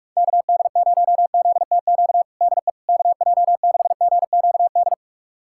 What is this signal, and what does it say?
Morse code, heard as a keyed tone, about 41 wpm, 700 Hz. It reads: KD0ZTQ DE KJ6GQD
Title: USA Callsign DE Callsign 45wpm